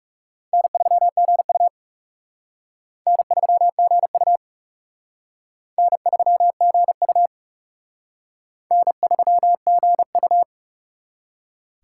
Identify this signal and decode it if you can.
Morse code, heard as a keyed tone, about 37 wpm, 700 Hz. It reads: N3GU N3GU N3GU N3GU
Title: Speed Racing - USA 1x2 Call Signs 30wpm